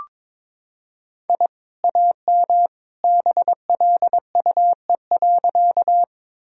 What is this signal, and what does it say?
Morse code, heard as a keyed tone, about 22 wpm, 700 Hz. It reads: I AM BLUE.
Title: Sentences from Top 1000 Words - Review Part 3 - 22wpm